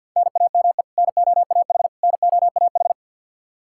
Morse code, heard as a keyed tone, about 52 wpm, 700 Hz. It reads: NAME NOAH NOAH
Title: Practice QSO - Exchange 1 - Beginning - 50wpm